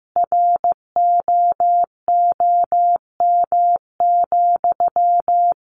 Morse code, heard as a keyed tone, about 15 wpm, 700 Hz. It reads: ROOM,
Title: Sentences from Top 600 Words 15wpm